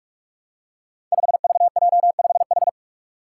Morse code, heard as a keyed tone, about 44 wpm, 700 Hz. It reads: 5VJ5H
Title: Five Challenging Letter-Number Combinations 45wpm